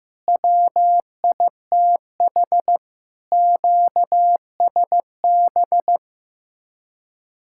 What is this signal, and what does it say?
Morse code, heard as a keyed tone, about 15 wpm, 700 Hz. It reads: WITH QSB
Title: Speed-Racing Common QSO Phrases Encore - 15wpm